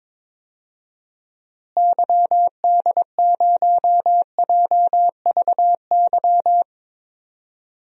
Morse code, heard as a keyed tone, about 22 wpm, 700 Hz. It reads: YD0JVY
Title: DXCC-100 Call Signs - 6 Characters - Repeated 2x - 22wpm